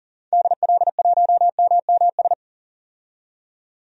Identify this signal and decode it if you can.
Morse code, heard as a keyed tone, about 40 wpm, 700 Hz. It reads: DL1MMS
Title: DXCC-100 Call Signs - 6 Characters - 40wpm